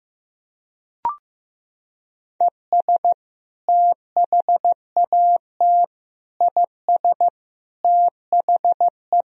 Morse code, heard as a keyed tone, about 15 wpm, 700 Hz. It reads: ES THAT IS THE
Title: Sentences from Top 400 Words and Abbreviations 15wpm